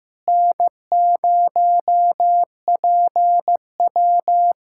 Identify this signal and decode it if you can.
Morse code, heard as a keyed tone, about 15 wpm, 700 Hz. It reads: N0PW
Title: USA Callsign DE Callsign - 4 Characters - 15wpm